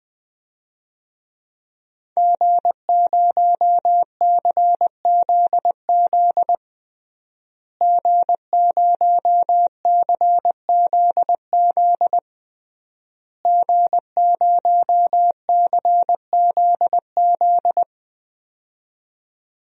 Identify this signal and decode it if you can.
Morse code, heard as a keyed tone, about 20 wpm, 700 Hz. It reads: G0CZZ G0CZZ G0CZZ
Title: DXCC-100 Call Signs - 5 Characters - Repeated 3x - 20wpm